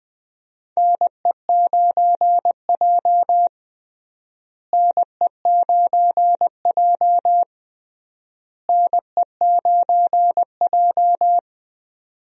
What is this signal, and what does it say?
Morse code, heard as a keyed tone, about 20 wpm, 700 Hz. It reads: NE9J NE9J NE9J
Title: USA Call Signs - 4 Characters - Repeated 3x - 20wpm